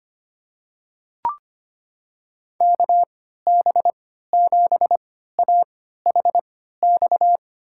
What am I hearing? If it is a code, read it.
Morse code, heard as a keyed tone, about 25 wpm, spaced as for 14 wpm, 700 Hz. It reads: KB7AHX
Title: USA Call Signs - 6 Characters - 3x Character Spacing - 25wpm